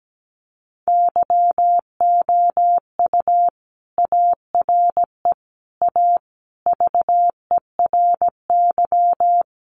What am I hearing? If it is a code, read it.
Morse code, heard as a keyed tone, about 17 wpm, 700 Hz. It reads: YOU ARE A VERY
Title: Sentences from Top 1000 Words 17wpm